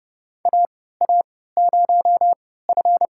Morse code, heard as a keyed tone, about 32 wpm, spaced as for 17 wpm, 700 Hz. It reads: AA0F
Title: USA Call Signs - 4 Characters - 3x Character Spacing - 30wpm